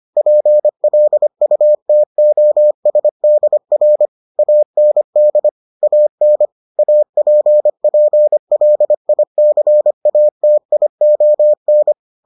Morse code, heard as a keyed tone, about 25 wpm, 600 Hz. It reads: PLUTOSDR AND AN APPLICATION